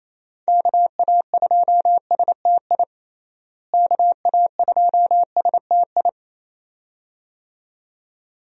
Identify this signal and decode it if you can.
Morse code, heard as a keyed tone, about 28 wpm, 700 Hz. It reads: KA2STI KA2STI
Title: USA Call Signs - 6 Characters - Repeated 2x - 28wpm